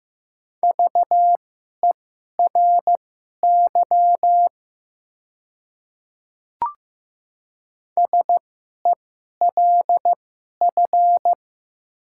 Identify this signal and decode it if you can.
Morse code, heard as a keyed tone, about 15 wpm, spaced as for 11 wpm, 700 Hz. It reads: VERY SELF
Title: 2x Character Spacing - Top 200 Words - 15wpm